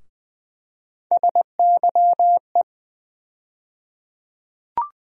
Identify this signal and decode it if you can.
Morse code, heard as a keyed tone, about 20 wpm, 700 Hz. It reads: SYE
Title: USA Call Sign Postfixes - 3 Letters 20wpm